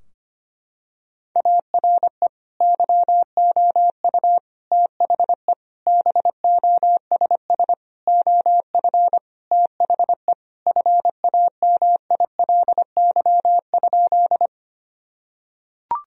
Morse code, heard as a keyed tone, about 25 wpm, 700 Hz. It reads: ARE YOU THE BOSS OF THE FAMILY?
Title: Sentences from Top 1000 Words - Review Part 4 - 25wpm